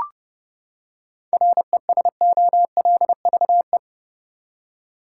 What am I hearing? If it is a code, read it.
Morse code, heard as a keyed tone, about 30 wpm, 700 Hz. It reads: RESOLVE
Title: Seven Letter Words - 30wpm